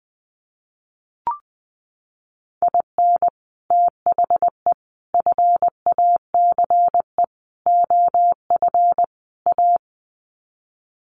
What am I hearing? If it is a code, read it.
Morse code, heard as a keyed tone, about 20 wpm, 700 Hz. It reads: IN THE FACE OF A
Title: Sets of 5 Words Encore 20wpm